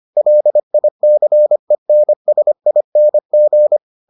Morse code, heard as a keyed tone, about 25 wpm, 600 Hz. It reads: LICENSING